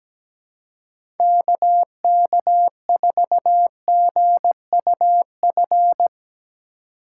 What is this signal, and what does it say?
Morse code, heard as a keyed tone, about 17 wpm, 700 Hz. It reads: KK4GUF
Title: USA Call Signs - 3x Character Spacing - 17wpm